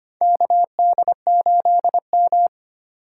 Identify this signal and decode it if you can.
Morse code, heard as a keyed tone, about 25 wpm, 700 Hz. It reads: KD8M
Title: USA Call Signs - 4 Characters - 4x Character Spacing - 25wpm